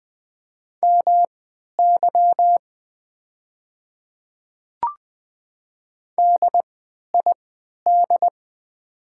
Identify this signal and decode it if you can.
Morse code, heard as a keyed tone, about 20 wpm, spaced as for 11 wpm, 700 Hz. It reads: MY DID
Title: 3x Character Spacing - Top 100 Words - 20wpm